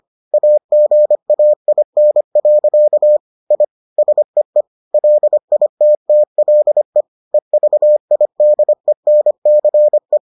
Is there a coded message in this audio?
Yes: Morse code from 0.3 to 10.2 s, heard as a keyed tone, about 25 wpm, 600 Hz. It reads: AGAIN. I SEE LITTLE EVIDENCE